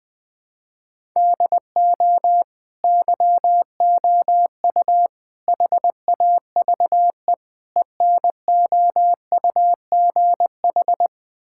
Morse code, heard as a keyed tone, about 20 wpm, 700 Hz. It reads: DO YOU HAVE ENOUGH
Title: Sentences from Top 1000 Words - Review Part 5 - 20wpm